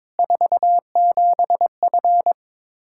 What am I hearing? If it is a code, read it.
Morse code, heard as a keyed tone, about 22 wpm, 700 Hz. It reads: 47F
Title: Practice QSO - Exchange 2 - Info - 22wpm